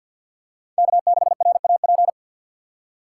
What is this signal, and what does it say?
Morse code, heard as a keyed tone, about 52 wpm, 700 Hz. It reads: K6RAP
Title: USA Call Signs - 5 Characters - 3x Character Spacing - 50wpm